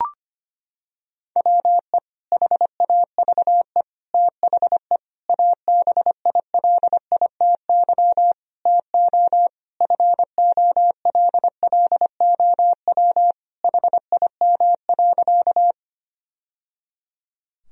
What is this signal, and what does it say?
Morse code, heard as a keyed tone, about 25 wpm, 700 Hz. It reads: WE HAVE THE ABILITY TO FOLLOW HIM.
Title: Sentences from Top 1000 Words - Review Part 3 - 25wpm